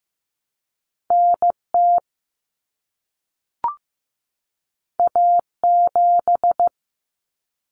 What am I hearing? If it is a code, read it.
Morse code, heard as a keyed tone, about 15 wpm, 700 Hz. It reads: NT A7
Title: Common Cut Numbers - 2-Digit 15wpm